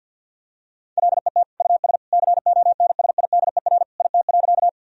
Morse code, heard as a keyed tone, about 50 wpm, 700 Hz. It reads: LET US CONSIDER IT.
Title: Sentences from Top 1000 Words - Review Part 4 - 50wpm